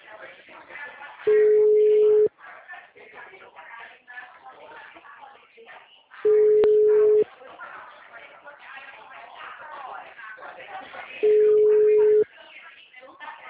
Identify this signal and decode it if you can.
Ringback tone (Europe, 425 Hz), with speech.
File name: out- -2015- - - .185